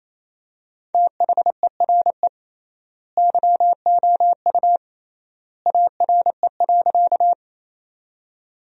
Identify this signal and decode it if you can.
Morse code, heard as a keyed tone, about 28 wpm, 700 Hz. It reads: THERE YOU ARE.
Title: 3x Word-Spacing - Sentences from Top 100 Words - 28wpm